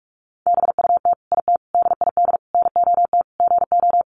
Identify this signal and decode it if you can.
Morse code, heard as a keyed tone, about 44 wpm, 700 Hz. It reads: BUT IT DID NOT GO
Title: Sentences from Top 600 Words 45wpm